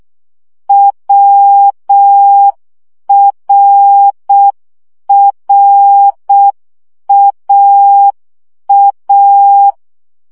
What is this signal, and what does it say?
Morse code, heard as a keyed tone, about 6 wpm, 800 Hz. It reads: WRRAA
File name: CW gyakorlas a r w 5 baud